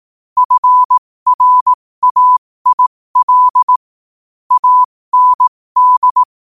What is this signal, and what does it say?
Morse code, heard as a keyed tone, about 18 wpm, spaced as for 15 wpm, 1000 Hz. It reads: FRAIL AND
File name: qotdfiles / 15wpm